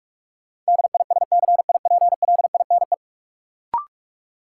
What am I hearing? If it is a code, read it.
Morse code, heard as a keyed tone, about 46 wpm, 700 Hz. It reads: DISCIPLINE
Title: Sentences from Top 900 Words - New Word Drill - 45wpm